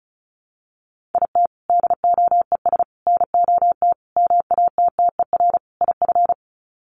Morse code, heard as a keyed tone, about 34 wpm, 700 Hz. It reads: IT DOES NOT MATTER IF
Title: Sets of 5 Words Encore 35wpm